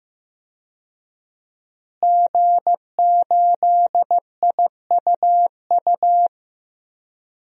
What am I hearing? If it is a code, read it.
Morse code, heard as a keyed tone, about 15 wpm, 700 Hz. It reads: G8IUU
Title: DXCC-100 Call Signs - 5 Characters - Repeated 2x - 15wpm